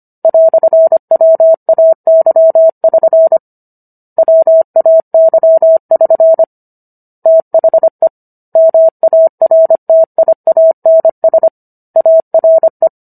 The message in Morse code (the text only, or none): "WAY<SN> WAY<SN> THE MARTIANS ARE